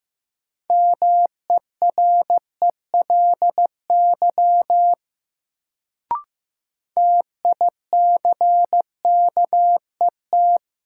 Morse code, heard as a keyed tone, about 15 wpm, 700 Hz. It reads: MERELY TICKET
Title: Sentences from Top 1400 Words - New Word Drill - 15wpm